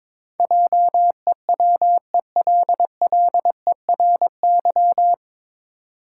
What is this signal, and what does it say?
Morse code, heard as a keyed tone, about 22 wpm, 700 Hz. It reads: JEWELLERY
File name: 9 Letter Words - First 3 Letters Given - 22wpm